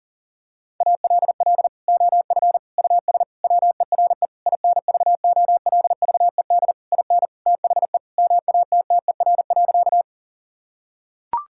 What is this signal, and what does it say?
Morse code, heard as a keyed tone, about 40 wpm, 700 Hz. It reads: ALL OF US WERE INVOLVED IN THE MATTER.
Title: Sentences from Top 1000 Words - Review Part 5 - 40wpm